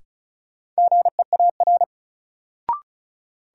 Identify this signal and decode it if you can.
Morse code, heard as a keyed tone, about 34 wpm, 700 Hz. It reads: GEAR
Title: Sentences from Top 1400 Words - New Word Drill Alt - 35wpm